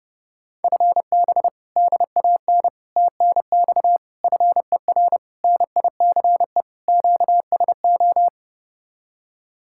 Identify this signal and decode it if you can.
Morse code, heard as a keyed tone, about 30 wpm, 700 Hz. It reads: FB DAN TNX FER NICE QSO